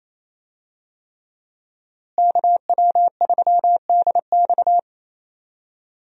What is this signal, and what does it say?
Morse code, heard as a keyed tone, about 28 wpm, 700 Hz. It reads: KW3DX
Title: USA Call Signs - 5 Characters - Repeated 3x - 28wpm